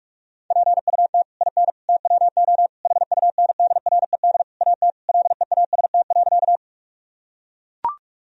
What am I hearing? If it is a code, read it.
Morse code, heard as a keyed tone, about 41 wpm, 700 Hz. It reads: PUT IN TWO HUNDRED AT LEAST.
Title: Sentences from Top 1000 Words - Review Part 2 - 45wpm